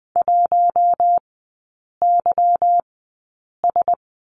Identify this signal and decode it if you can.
Morse code, heard as a keyed tone, about 20 wpm, spaced as for 8 wpm, 700 Hz. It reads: 1YS
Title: 172 Lesson 28e - Number 9 - Three Characters 20wpm